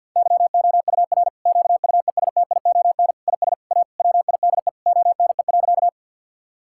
Morse code, heard as a keyed tone, about 50 wpm, 700 Hz. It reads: YOUR QUESTION IS A WIDE ONE.
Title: Sentences from Top 1000 Words - Review Part 6 - 50wpm